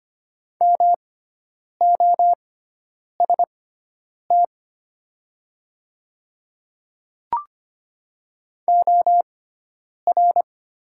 Morse code, heard as a keyed tone, about 25 wpm, spaced as for 9 wpm, 700 Hz. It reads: MOST OR
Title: 6x Character Spacing - Top 200 Words - 25wpm